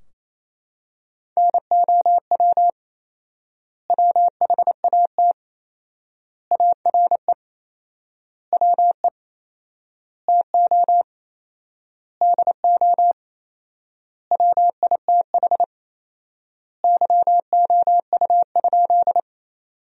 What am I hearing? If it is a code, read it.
Morse code, heard as a keyed tone, about 27 wpm, 700 Hz. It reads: NOW WHAT ARE WE TO DO WITH YOU?
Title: 4x Word-Spacing - Sentences from Top 100 Words - 28wpm